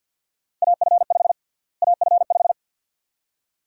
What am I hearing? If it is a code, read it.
Morse code, heard as a keyed tone, about 49 wpm, 700 Hz. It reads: AF5 AF5
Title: USA Call Sign Prefixes - Repeated 2x - 50wpm